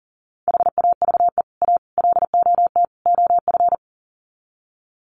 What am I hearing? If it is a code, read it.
Morse code, heard as a keyed tone, about 40 wpm, 700 Hz. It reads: HAVE A LOT OF
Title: Sets of 4 Words 40wpm